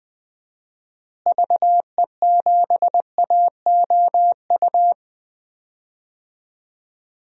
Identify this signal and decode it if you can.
Morse code, heard as a keyed tone, about 21 wpm, 700 Hz. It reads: VE7AOU
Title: DXCC-100 Call Signs - 6 Characters - Repeated 2x - 20wpm